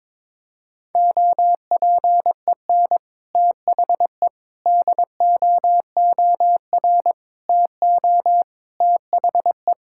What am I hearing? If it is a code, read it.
Morse code, heard as a keyed tone, about 21 wpm, 700 Hz. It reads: OPEN THE DOOR TO THE
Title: Sentences from Top 1000 Words - Review Part 6 - 22wpm